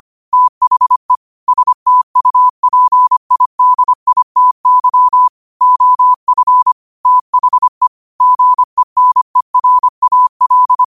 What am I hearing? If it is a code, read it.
Morse code, heard as a keyed tone, about 25 wpm, 1000 Hz. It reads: THE STUPIDITY OF THE GENERAL